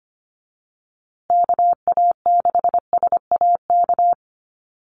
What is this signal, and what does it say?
Morse code, heard as a keyed tone, about 25 wpm, 700 Hz. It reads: KA6SAK
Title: USA Call Signs Repeated 2x 25wpm